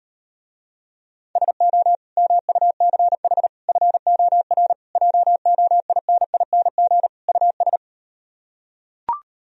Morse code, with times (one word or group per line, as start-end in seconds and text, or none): 1.4-2.0 SO
2.2-3.5 MUCH
3.7-4.7 FOR
5.0-7.1 JOINING
7.3-7.8 US